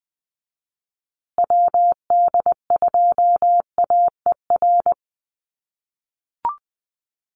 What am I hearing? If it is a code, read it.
Morse code, heard as a keyed tone, about 20 wpm, 700 Hz. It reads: WD2AER
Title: USA Call Signs Repeated 3x 20wpm